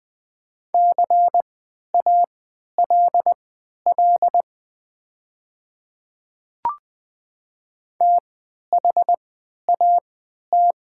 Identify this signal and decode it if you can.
Morse code, heard as a keyed tone, about 20 wpm, spaced as for 11 wpm, 700 Hz. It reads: CALL THAT
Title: 3x Character Spacing - Top 200 Words - 20wpm